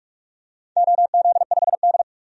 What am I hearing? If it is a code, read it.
Morse code, heard as a keyed tone, about 44 wpm, 700 Hz. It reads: OZ5D
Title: DXCC-100 Call Signs - 4 Characters - 45wpm